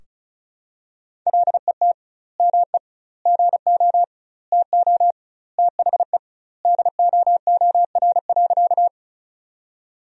Morse code, heard as a keyed tone, about 36 wpm, 700 Hz. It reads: LET ME GO TO THE DOOR.
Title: 2x Word-Spacing - Sentences from Top 300 Words - 35wpm